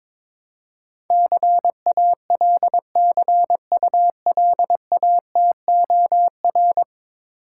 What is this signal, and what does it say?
Morse code, heard as a keyed tone, about 22 wpm, 700 Hz. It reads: CALCULATOR
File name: Difficult Words with Hints - 22wpm